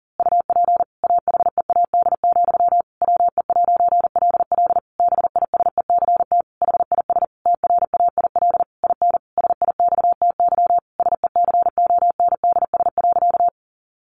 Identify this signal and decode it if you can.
Morse code, heard as a keyed tone, about 40 wpm, 700 Hz. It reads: UP AHEAD, WE'LL BISECT HIS TRAIL IN SIXTY SECONDS.
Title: A Sound Of Thunder 40wpm